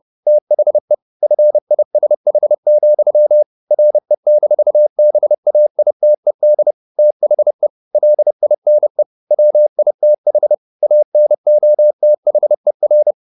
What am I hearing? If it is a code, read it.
Morse code, heard as a keyed tone, about 30 wpm, 600 Hz. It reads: THE FISH, RE-BAITED THE LINE WITH ANOTHER